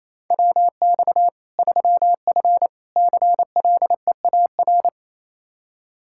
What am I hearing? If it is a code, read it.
Morse code, heard as a keyed tone, about 28 wpm, 700 Hz. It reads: WX 3F CLEAR
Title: K1USN SST QSO Practice 28wpm